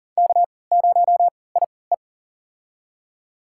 Morse code, heard as a keyed tone, about 37 wpm, spaced as for 22 wpm, 700 Hz. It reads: K0IE
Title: USA Call Signs - 4 Characters - 3x Character Spacing - 40wpm